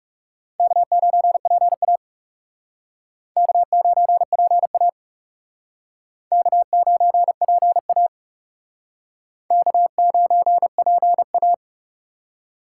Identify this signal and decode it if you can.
Morse code, heard as a keyed tone, about 38 wpm, 700 Hz. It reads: K9PA K9PA K9PA K9PA